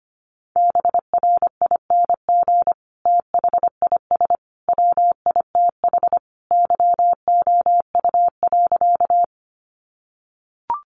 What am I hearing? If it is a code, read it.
Morse code, heard as a keyed tone, about 25 wpm, 700 Hz. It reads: BRING THIS WITH YOU.
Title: Sentences from Top 500 Words 25wpm